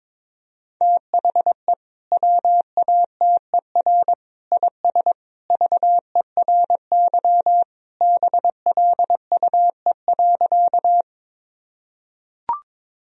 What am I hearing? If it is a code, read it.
Morse code, heard as a keyed tone, about 21 wpm, 700 Hz. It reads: THE WATER IS VERY BLUE.
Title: Sentences from Top 1000 Words - Review Part 3 - 22wpm